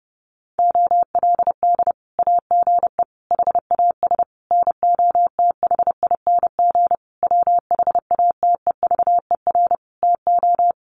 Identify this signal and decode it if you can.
Morse code, heard as a keyed tone, about 28 wpm, 700 Hz. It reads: OLD AGE HAS NOTHING WHATEVER TO
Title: Sentences from Top 1100 Words 30wpm